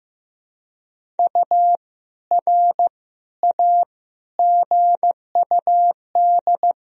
Morse code, heard as a keyed tone, about 15 wpm, 700 Hz. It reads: U R A GUD
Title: Sentences from Top 400 Words and Abbreviations 15wpm